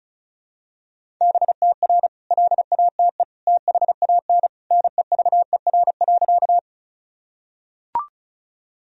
Morse code, heard as a keyed tone, about 34 wpm, 700 Hz. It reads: BTR LATE THAN NEVER.
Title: Sentences from Top 400 Words and Abbreviations 35wpm